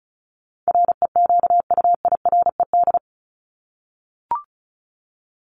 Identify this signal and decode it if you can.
Morse code, heard as a keyed tone, about 34 wpm, 700 Hz. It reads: REQUIRED
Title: ED Suffix - Top 100 Words with Base Words 35wpm